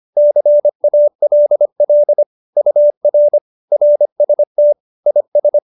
Morse code, heard as a keyed tone, about 25 wpm, 590 Hz. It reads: CALL UR RST IS
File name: QSO303 25